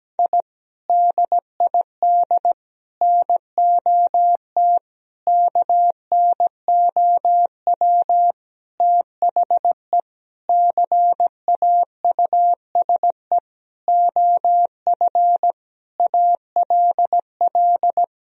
Morse code, heard as a keyed tone, about 17 wpm, 700 Hz. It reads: I DID NOT KNOW THE CAUSE OF ALL
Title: Sentences from Top 1000 Words - Review Part 1 - 17wpm